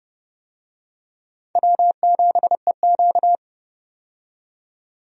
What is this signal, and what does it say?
Morse code, heard as a keyed tone, about 29 wpm, 700 Hz. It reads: W7EQ
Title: USA Call Signs - 4 Characters - 30wpm